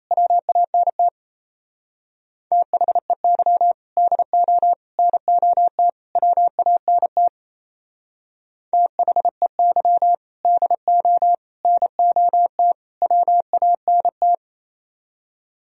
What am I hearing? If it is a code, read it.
Morse code, heard as a keyed tone, about 35 wpm, 700 Hz. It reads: WANT THEY DO NOT WANT THEY DO NOT WANT